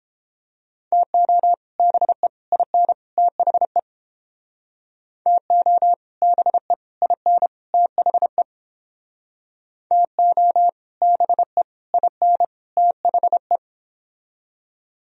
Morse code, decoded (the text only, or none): TO BE IN THE TO BE IN THE TO BE IN THE